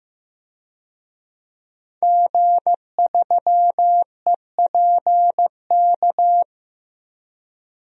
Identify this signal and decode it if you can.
Morse code, heard as a keyed tone, about 15 wpm, 700 Hz. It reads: G3EPK
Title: DXCC-100 Call Signs - 5 Characters - 15wpm